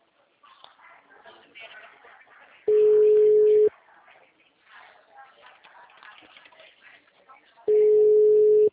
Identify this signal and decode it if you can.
Ringback tone (Europe, 425 Hz), with speech.